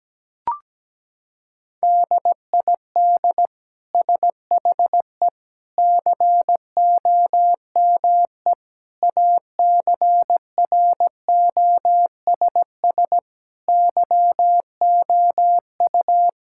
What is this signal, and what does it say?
Morse code, heard as a keyed tone, about 17 wpm, 700 Hz. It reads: DID SHE COME ACROSS YOU
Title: Sentences from Top 1000 Words - Review Part 5 - 17wpm